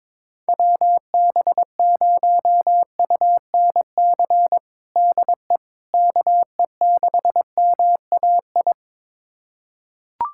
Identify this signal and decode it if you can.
Morse code, heard as a keyed tone, about 22 wpm, 700 Hz. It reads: WB0UNC DE KE6MAI
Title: USA Callsign DE Callsign 22wpm